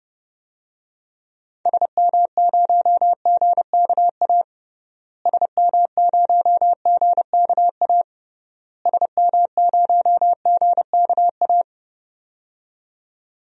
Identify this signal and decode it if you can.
Morse code, heard as a keyed tone, about 30 wpm, 700 Hz. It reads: SM0GKA SM0GKA SM0GKA
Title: DXCC-100 Call Signs - 6 Characters - Repeated 3x - 30wpm